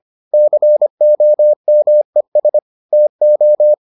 Morse code, heard as a keyed tone, about 25 wpm, 600 Hz. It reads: COMES TO